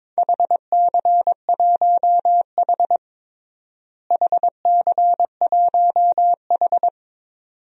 Morse code, heard as a keyed tone, about 22 wpm, 700 Hz. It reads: HC1H HC1H
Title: DXCC-100 Call Signs - 4 Characters - Repeated 2x - 22wpm